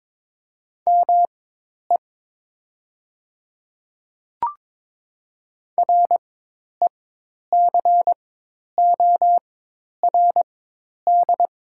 Morse code, heard as a keyed tone, about 22 wpm, spaced as for 10 wpm, 700 Hz. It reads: ME RECORD
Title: 4x Character Spacing - Top 500 Words - 22wpm